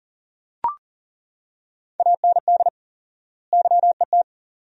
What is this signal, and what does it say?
Morse code, heard as a keyed tone, about 40 wpm, 700 Hz. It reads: AND YET
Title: 4x Word-Spacing - Sentences from Top 500 Words - 40wpm